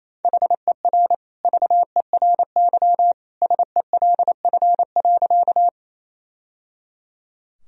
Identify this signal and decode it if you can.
Morse code, heard as a keyed tone, about 28 wpm, 700 Hz. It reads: HER VERY SELF.
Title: Sentences from Top 1000 Words - Review Part 1 - 28wpm